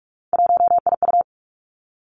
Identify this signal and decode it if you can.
Morse code, heard as a keyed tone, about 46 wpm, 700 Hz. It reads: 1IU
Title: Three Challenging Letter-Number Combinations 45wpm